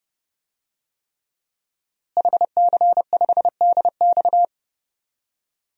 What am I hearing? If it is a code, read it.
Morse code, heard as a keyed tone, about 32 wpm, 700 Hz. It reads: HC5DX
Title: DXCC-100 Call Signs - 5 Characters - Repeated 3x - 30wpm